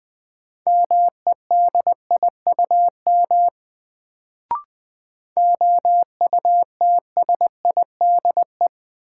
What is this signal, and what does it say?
Morse code, heard as a keyed tone, about 20 wpm, 700 Hz. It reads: MEDIUM OUTSIDE
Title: Sentences from Top 900 Words - New Word Drill - 20wpm